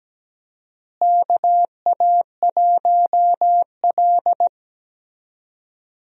Morse code, heard as a keyed tone, about 17 wpm, 700 Hz. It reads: KA1L
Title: USA Call Signs - 4 Characters - Repeated 2x - 17wpm